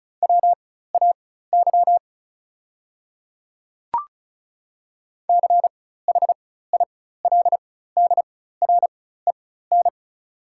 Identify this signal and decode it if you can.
Morse code, heard as a keyed tone, about 36 wpm, spaced as for 16 wpm, 690 Hz. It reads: WAY CHILDREN
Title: 4x Character Spacing - Top 300 Words - 35wpm